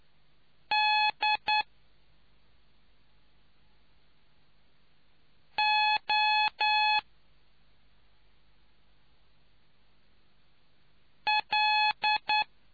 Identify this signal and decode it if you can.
Morse code, heard as a keyed tone, about 9 wpm, 850 Hz. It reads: D O L